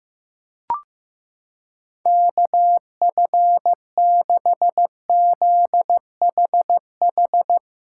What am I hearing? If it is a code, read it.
Morse code, heard as a keyed tone, about 15 wpm, 700 Hz. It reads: KF6ZHH
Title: USA Callsign DE Callsign 15wpm